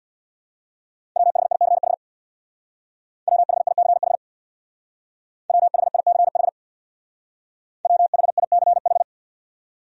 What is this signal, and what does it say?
Morse code, heard as a keyed tone, about 61 wpm, 690 Hz. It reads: WHICH WHICH WHICH WHICH
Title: Speed-Racing Top 500 Words 50wpm